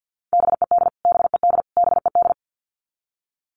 Morse code, heard as a keyed tone, about 50 wpm, 700 Hz. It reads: BED BED BED
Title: Top 500 Words Repeated 50wpm